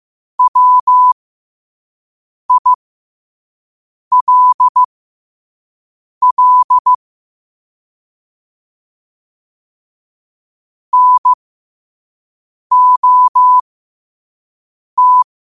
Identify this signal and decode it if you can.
Morse code, heard as a keyed tone, about 15 wpm, spaced as for 5 wpm, 1000 Hz. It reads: WILL NOT
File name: qotdfiles / 5wpm